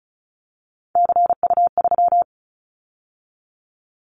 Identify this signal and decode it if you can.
Morse code, heard as a keyed tone, about 36 wpm, 700 Hz. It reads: CU3
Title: Callsign Prefixes for Top 50 DXCC - 35wpm